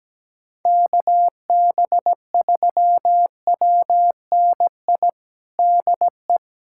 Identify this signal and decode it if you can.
Morse code, heard as a keyed tone, about 17 wpm, 700 Hz. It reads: KB3WNI DE
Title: USA Callsign DE Callsign 17wpm